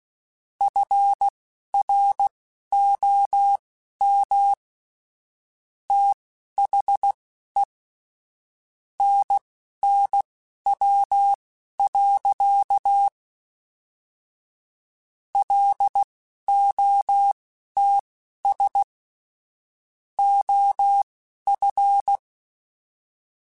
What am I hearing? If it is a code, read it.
Morse code, heard as a keyed tone, about 16 wpm, spaced as for 12 wpm, 800 Hz. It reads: FROM THE NNW. LOTS OF
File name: Weather Sets / Weather Sets-mp3s / Weather 5 12 @10